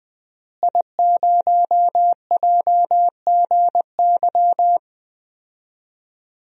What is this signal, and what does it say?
Morse code, heard as a keyed tone, about 21 wpm, 700 Hz. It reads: I0JGY
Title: DXCC-100 Call Signs - 5 Characters - Repeated 2x - 20wpm